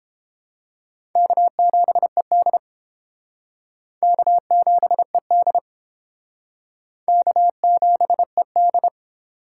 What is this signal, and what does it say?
Morse code, heard as a keyed tone, about 29 wpm, 700 Hz. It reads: K7ED K7ED K7ED